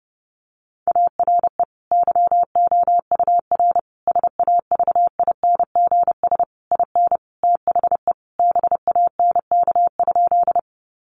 Morse code, heard as a keyed tone, about 28 wpm, 700 Hz. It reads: ARE YOUR SAVINGS IN THE BANK?
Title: Sentences from Top 1200 Words 30wpm